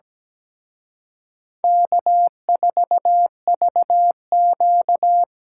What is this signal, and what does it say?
Morse code, heard as a keyed tone, about 20 wpm, 700 Hz. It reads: K4VQ
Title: Speed Racing - USA 1x2 Call Signs 17wpm